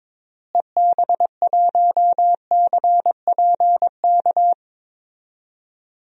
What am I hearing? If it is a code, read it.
Morse code, heard as a keyed tone, about 22 wpm, 700 Hz. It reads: EB1CPK
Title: DXCC-100 Call Signs - 6 Characters - Repeated 3x - 22wpm